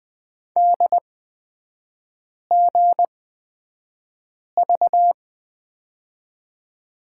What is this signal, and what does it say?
Morse code, heard as a keyed tone, about 20 wpm, spaced as for 5 wpm, 700 Hz. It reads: DGV
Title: 250 Lesson 40e - BK - Three Characters 20wpm